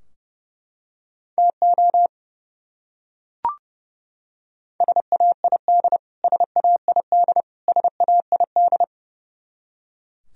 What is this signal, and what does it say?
Morse code, heard as a keyed tone, about 30 wpm, 700 Hz. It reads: TO SAID SAID SAID
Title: Top 100 Words Repeated 30wpm